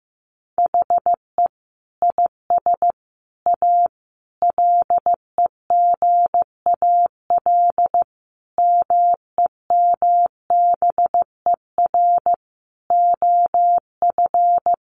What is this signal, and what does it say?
Morse code, heard as a keyed tone, about 15 wpm, 700 Hz. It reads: HE IS A LEGAL MEMBER OF
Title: Sentences from Top 1200 Words 15wpm